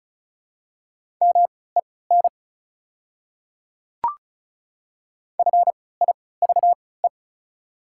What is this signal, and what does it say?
Morse code, heard as a keyed tone, about 36 wpm, spaced as for 20 wpm, 690 Hz. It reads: MEN FIVE
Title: 3x Character Spacing - Top 500 Words - 35wpm